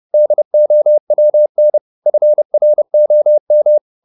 Morse code, heard as a keyed tone, about 30 wpm, 600 Hz. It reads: DOWN FROM